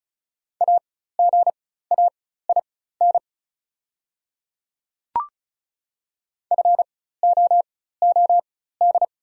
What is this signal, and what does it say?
Morse code, heard as a keyed tone, about 34 wpm, spaced as for 16 wpm, 700 Hz. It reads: AGAIN FOOD
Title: 4x Character Spacing - Top 300 Words - 35wpm